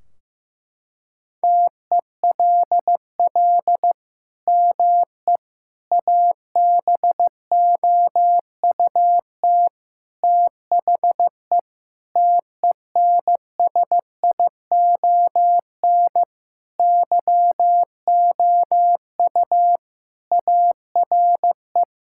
Morse code, heard as a keyed tone, about 16 wpm, 700 Hz. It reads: TELL ME ABOUT THE TENSION YOU ARE
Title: Sentences from Top 1600 Words 15wpm